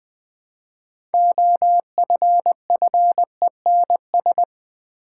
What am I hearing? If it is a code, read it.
Morse code, heard as a keyed tone, about 20 wpm, 700 Hz. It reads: OFFENS
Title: Build A Word - 3 Letters At A Time - 9-Letter Words - 20wpm